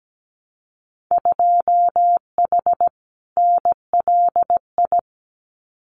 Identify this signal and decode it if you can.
Morse code, heard as a keyed tone, about 17 wpm, 700 Hz. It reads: 2H NLI
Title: Winter Field Day Exchanges Encore 17wpm